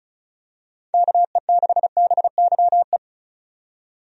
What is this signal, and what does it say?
Morse code, heard as a keyed tone, about 34 wpm, 700 Hz. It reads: KE6BYE
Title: USA Call Signs - 6 Characters - 35wpm